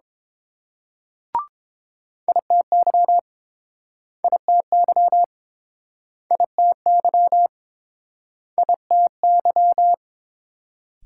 Morse code, heard as a keyed tone, about 31 wpm, 700 Hz. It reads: ITY ITY ITY ITY